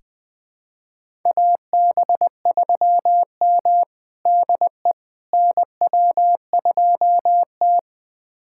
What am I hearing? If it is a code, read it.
Morse code, heard as a keyed tone, about 20 wpm, 700 Hz. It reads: AB3M DE NW2T
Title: USA Callsign DE Callsign - 4 Characters - 20wpm